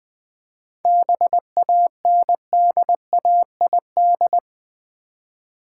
Morse code, heard as a keyed tone, about 20 wpm, 700 Hz. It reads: BANDAID
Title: Easy Words with Hints - 20wpm